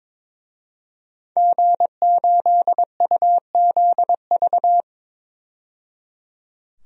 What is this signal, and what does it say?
Morse code, heard as a keyed tone, about 22 wpm, 700 Hz. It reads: G8UZV
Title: DXCC-100 Call Signs - 5 Characters - 22wpm